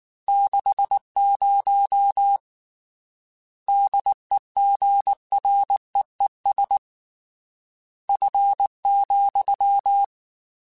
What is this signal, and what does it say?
Morse code, heard as a keyed tone, about 19 wpm, 800 Hz. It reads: 60 DEGREES F,
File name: Weather Sets / Weather Sets-mp3s / Weather 5 22 @18